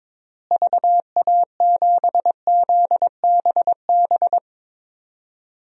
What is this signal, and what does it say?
Morse code, heard as a keyed tone, about 22 wpm, 700 Hz. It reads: VA7ZBB
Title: DXCC-100 Call Signs - 6 Characters - Repeated 2x - 22wpm